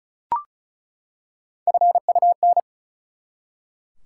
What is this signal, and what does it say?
Morse code, heard as a keyed tone, about 34 wpm, 700 Hz. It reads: FUN
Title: Three Letter Words - 35wpm